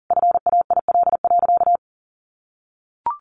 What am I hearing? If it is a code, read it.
Morse code, heard as a keyed tone, about 40 wpm, 700 Hz. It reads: FAIL.
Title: Sentences from Top 1100 Words 40wpm